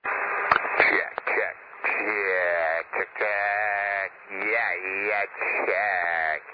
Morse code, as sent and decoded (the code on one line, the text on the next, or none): none